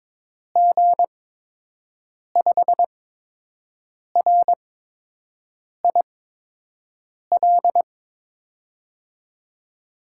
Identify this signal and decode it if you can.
Morse code, heard as a keyed tone, about 22 wpm, spaced as for 6 wpm, 700 Hz. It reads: G5RIL
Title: USA Call Signs - 6 Characters - 8x Character Spacing - 22wpm